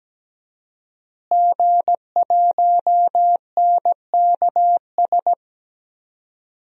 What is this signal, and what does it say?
Morse code, heard as a keyed tone, about 17 wpm, 700 Hz. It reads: G1NKS
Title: DXCC-100 Call Signs - 5 Characters - Repeated 3x - 17wpm